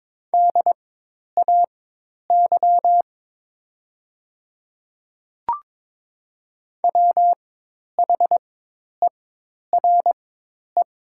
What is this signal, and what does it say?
Morse code, heard as a keyed tone, about 21 wpm, spaced as for 10 wpm, 700 Hz. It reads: DAY WHERE
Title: 4x Character Spacing - Top 200 Words - 22wpm